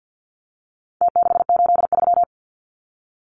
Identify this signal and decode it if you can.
Morse code, heard as a keyed tone, about 50 wpm, 700 Hz. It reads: T683
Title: Numbers - Contest Style - 4-Digit 50wpm